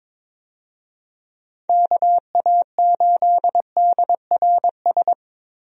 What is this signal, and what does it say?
Morse code, heard as a keyed tone, about 23 wpm, 700 Hz. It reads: KA8DRS
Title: USA Call Signs - 6 Characters - Repeated 2x - 22wpm